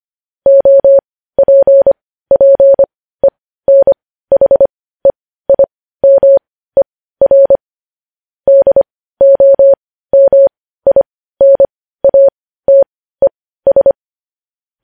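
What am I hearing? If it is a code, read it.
Morse code, heard as a keyed tone, about 25 wpm, spaced as for 15 wpm, 550 Hz. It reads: OPPENHEIMER DOMINATES
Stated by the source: Morse practice of news headlines at 15 wpm.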